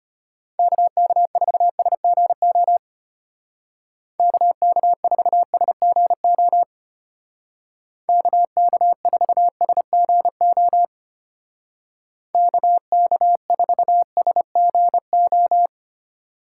Morse code, heard as a keyed tone, about 32 wpm, 700 Hz. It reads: KK4SGO KK4SGO KK4SGO KK4SGO